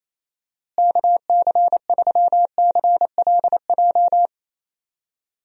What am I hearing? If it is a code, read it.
Morse code, heard as a keyed tone, about 27 wpm, 700 Hz. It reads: KC3CLJ
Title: USA Call Signs - 6 Characters - Repeated 3x - 28wpm